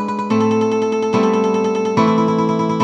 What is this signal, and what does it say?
There is no code.